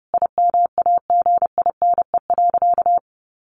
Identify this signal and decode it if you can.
Morse code, heard as a keyed tone, about 30 wpm, 700 Hz. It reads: IMAGINE.
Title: Sentences from Top 1400 Words 30wpm